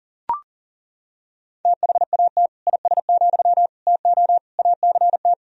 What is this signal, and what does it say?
Morse code, heard as a keyed tone, about 40 wpm, 700 Hz. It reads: THAT IS, TO ACT
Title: Sentences from Top 1000 Words - Review Part 4 - 40wpm